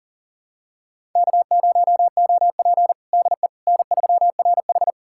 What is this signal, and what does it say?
Morse code, heard as a keyed tone, about 37 wpm, 700 Hz. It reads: K0OP DE N3RH
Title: USA Callsign DE Callsign - 4 Characters - 40wpm